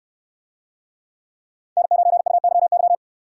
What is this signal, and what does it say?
Morse code, heard as a keyed tone, about 66 wpm, 700 Hz. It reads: N9UYY